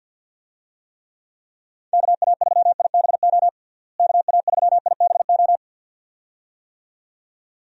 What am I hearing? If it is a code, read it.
Morse code, heard as a keyed tone, about 52 wpm, 700 Hz. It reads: KA3IBO KA3IBO
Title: USA Call Signs - 6 Characters - Repeated 2x - 50wpm